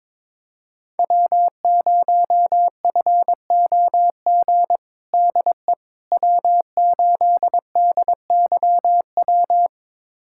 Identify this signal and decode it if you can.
Morse code, heard as a keyed tone, about 22 wpm, 700 Hz. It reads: W0FOG DE W8DYW
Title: USA Callsign DE Callsign - 5 Characters - 22wpm